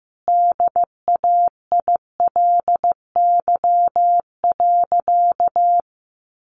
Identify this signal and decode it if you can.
Morse code, heard as a keyed tone, about 15 wpm, 700 Hz. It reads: DAILY.
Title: Sentences from Top 1100 Words 15wpm